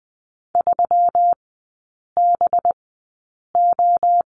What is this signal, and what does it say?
Morse code, heard as a keyed tone, about 19 wpm, spaced as for 8 wpm, 700 Hz. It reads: 3BO
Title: 184 Lesson 30e - Letter b - Three Characters 20wpm